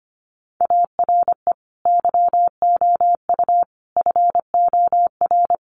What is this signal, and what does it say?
Morse code, heard as a keyed tone, about 25 wpm, 700 Hz. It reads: ARE YOU FOR
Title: Sentences from Top 1100 Words 25wpm